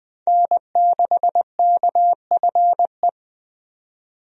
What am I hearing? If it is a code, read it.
Morse code, heard as a keyed tone, about 21 wpm, 700 Hz. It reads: N6KFE
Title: USA Call Signs - 5 Characters - 20wpm